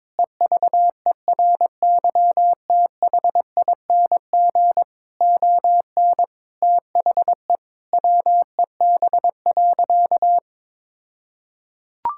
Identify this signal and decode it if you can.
Morse code, heard as a keyed tone, about 22 wpm, 700 Hz. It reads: EVERYTHING ON THE WEB.
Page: morse-code-ninja-practice-22wpm